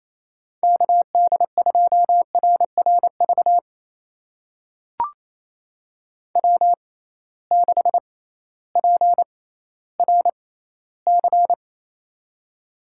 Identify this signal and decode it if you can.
Morse code, heard as a keyed tone, about 28 wpm, spaced as for 15 wpm, 700 Hz. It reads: KD2RRV W6PRC
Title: USA Call Signs - 6x Character Spacing - 28wpm